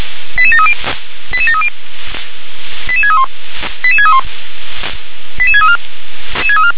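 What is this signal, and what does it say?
Five-tone selective call (ZVEI), heard as quick runs of five tones, tones of about 70 ms. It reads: 88730 88730 88631 88631 88634
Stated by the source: ZVEI five-tone selective call received with an SDR.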